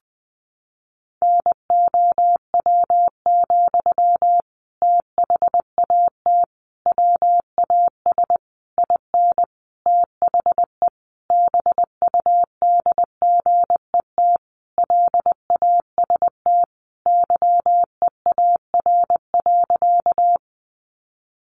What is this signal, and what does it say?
Morse code, heard as a keyed tone, about 20 wpm, 700 Hz. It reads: NOW, THAT WAS IN THE BUDGET LAST YEAR.
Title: Sentences from Top 1200 Words 20wpm